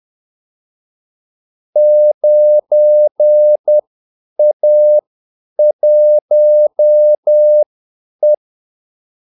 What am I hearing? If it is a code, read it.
Morse code, heard as a keyed tone, about 10 wpm, spaced as for 8 wpm, 600 Hz. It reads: 9A1E
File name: Callsigns-10-8wpm